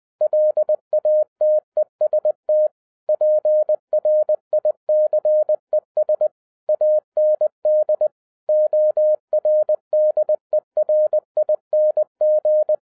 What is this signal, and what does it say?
Morse code, heard as a keyed tone, about 20 wpm, 610 Hz. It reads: LATEST PRICES AND ORDERING
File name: 2011-07-mp3 / n43 Czech Army Morse Keys 20wpm